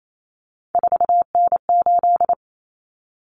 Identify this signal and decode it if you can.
Morse code, heard as a keyed tone, about 28 wpm, 700 Hz. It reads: 4N8
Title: Common Cut Numbers - 3-Digit 28wpm